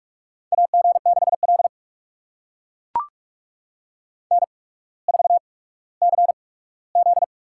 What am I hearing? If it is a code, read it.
Morse code, heard as a keyed tone, about 44 wpm, spaced as for 19 wpm, 700 Hz. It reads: AG6L N4CZ
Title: USA Call Signs - 4 Characters - 8x Character Spacing - 45wpm